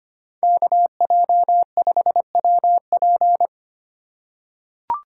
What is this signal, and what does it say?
Morse code, heard as a keyed tone, about 25 wpm, 700 Hz. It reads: KJ5WP
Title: USA Call Signs - 5 Characters - 6x Character Spacing - 25wpm